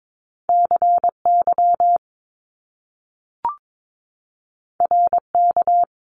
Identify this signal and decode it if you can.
Morse code, heard as a keyed tone, about 22 wpm, 700 Hz. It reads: CY RK
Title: Top Two Letter Combinations in Words 22wpm